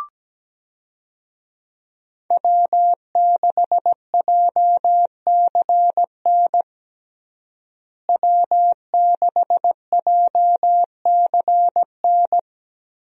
Morse code, heard as a keyed tone, about 17 wpm, 700 Hz. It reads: W6JCN W6JCN
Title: USA Call Signs - 5 Characters - Repeated 2x - 17wpm